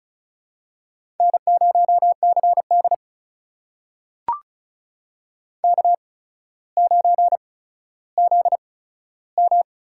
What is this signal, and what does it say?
Morse code, heard as a keyed tone, about 36 wpm, spaced as for 15 wpm, 700 Hz. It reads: N0CD K9ZM
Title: USA Call Signs - 4 Characters - 8x Character Spacing - 35wpm